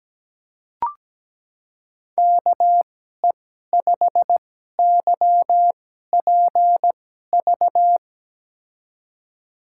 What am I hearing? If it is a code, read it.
Morse code, heard as a keyed tone, about 17 wpm, spaced as for 12 wpm, 700 Hz. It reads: KE5YPV
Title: USA Call Signs - 6 Characters - 2x Character Spacing - 17wpm